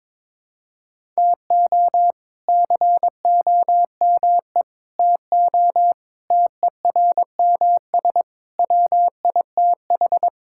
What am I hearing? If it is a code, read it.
Morse code, heard as a keyed tone, about 22 wpm, 700 Hz. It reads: TO COME TO TERMS WITH
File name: practice / 22wpm / Sets of 8 Words - 22wpm